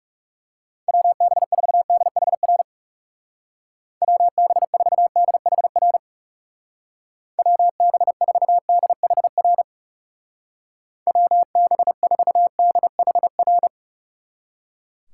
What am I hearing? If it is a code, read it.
Morse code, heard as a keyed tone, about 39 wpm, 700 Hz. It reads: WB4DHR WB4DHR WB4DHR WB4DHR